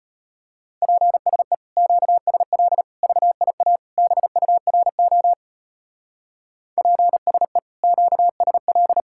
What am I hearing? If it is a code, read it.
Morse code, heard as a keyed tone, about 29 wpm, 700 Hz. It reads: PSE QSL VIA BURO PSE QSL
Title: Speed-Racing Common QSO Phrases - 25wpm